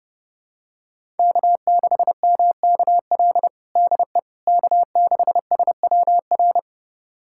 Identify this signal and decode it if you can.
Morse code, heard as a keyed tone, about 30 wpm, 700 Hz. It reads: K6MKL DE K6SWR
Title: USA Callsign DE Callsign - 5 Characters - 30wpm